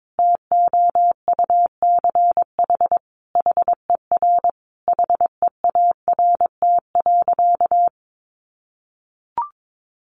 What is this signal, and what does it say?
Morse code, heard as a keyed tone, about 22 wpm, 700 Hz. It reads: TOUCH HER HEART.
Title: Sentences from Top 1200 Words 22wpm